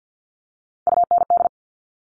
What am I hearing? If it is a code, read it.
Morse code, heard as a keyed tone, about 50 wpm, 700 Hz. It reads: UND
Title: Top Three Letter Combinations in Words 50wpm